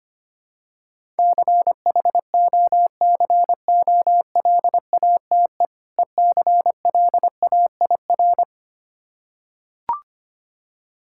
Morse code, heard as a keyed tone, about 25 wpm, 700 Hz. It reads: CHOCOLATE ECLAIR
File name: practice / 25wpm / Difficult Words with Hints - 25wpm